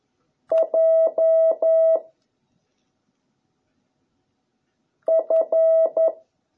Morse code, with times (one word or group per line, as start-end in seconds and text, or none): 0.5-2.0 J
5.1-6.1 F